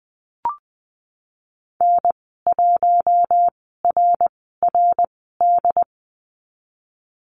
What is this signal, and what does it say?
Morse code, heard as a keyed tone, about 20 wpm, spaced as for 15 wpm, 700 Hz. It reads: N1RRD
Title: 076 Lesson 13g - Letter L - Call Signs 20wpm